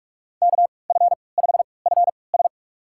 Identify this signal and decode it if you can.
Morse code, heard as a keyed tone, about 41 wpm, spaced as for 25 wpm, 700 Hz. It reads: KF5FS
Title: USA Call Signs - 5 Characters - 3x Character Spacing - 45wpm